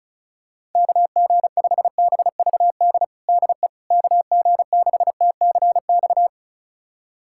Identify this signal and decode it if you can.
Morse code, heard as a keyed tone, about 34 wpm, 700 Hz. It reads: KG5BVD DE KG6TCX
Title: USA Callsign DE Callsign 35wpm